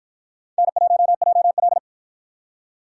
Morse code, heard as a keyed tone, about 50 wpm, 700 Hz. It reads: N1JL